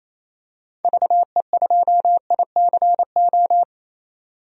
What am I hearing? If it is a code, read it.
Morse code, heard as a keyed tone, about 28 wpm, 700 Hz. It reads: VE2ICO
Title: DXCC-100 Call Signs - 6 Characters - 28wpm